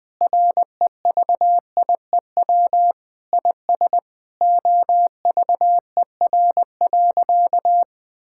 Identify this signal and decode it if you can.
Morse code, heard as a keyed tone, about 20 wpm, 700 Hz. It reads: REVIEW IS OVER.
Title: Sentences from Top 1000 Words - Review Part 6 - 20wpm